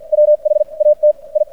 Morse code, heard as a keyed tone, about 56 wpm, spaced as for 40 wpm, 600 Hz. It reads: WHAT I
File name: archived / CRSnet